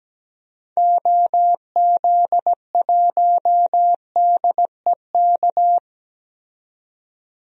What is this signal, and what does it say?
Morse code, heard as a keyed tone, about 17 wpm, 700 Hz. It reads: OZ1DEK
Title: DXCC-100 Call Signs - 6 Characters - 17wpm